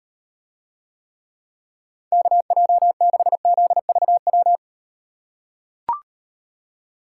Morse code, heard as a keyed tone, about 38 wpm, 700 Hz. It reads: KJ6ZVW